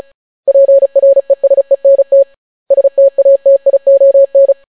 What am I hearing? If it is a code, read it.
Morse code, heard as a keyed tone, about 36 wpm, 550 Hz. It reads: PRESENT STATION